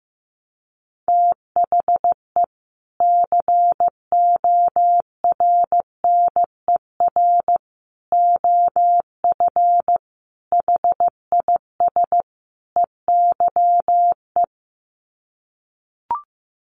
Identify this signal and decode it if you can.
Morse code, heard as a keyed tone, about 14 wpm, 700 Hz. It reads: THE CORNER OF HIS EYE
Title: Sets of 5 Words Encore2 15wpm